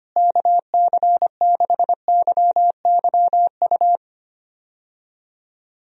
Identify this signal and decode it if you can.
Morse code, heard as a keyed tone, about 25 wpm, 700 Hz. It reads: KC6YYU
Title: USA Call Signs - 6 Characters - Repeated 2x - 25wpm